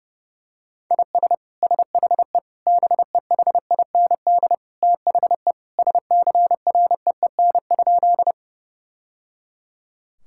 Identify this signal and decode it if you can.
Morse code, heard as a keyed tone, about 28 wpm, 700 Hz. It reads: IS SHE BEHIND THE SCREEN?
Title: Sentences from Top 1000 Words - Review Part 6 - 30wpm